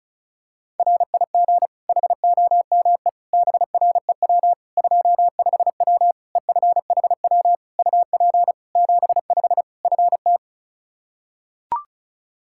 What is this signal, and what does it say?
Morse code, heard as a keyed tone, about 34 wpm, 700 Hz. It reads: RIG HOME BREW 25W EFHW UP 75 FT
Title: K1USN SST QSO Practice 35wpm